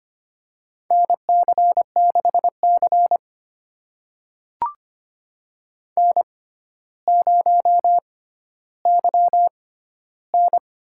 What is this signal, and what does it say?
Morse code, heard as a keyed tone, about 25 wpm, spaced as for 13 wpm, 700 Hz. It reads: NC6C N0YN
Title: USA Call Signs - 4 Characters - 6x Character Spacing - 25wpm